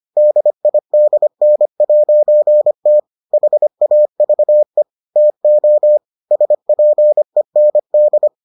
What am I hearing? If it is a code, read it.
Morse code, heard as a keyed tone, about 25 wpm, 600 Hz. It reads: DIDN'T HAVE TO SPEND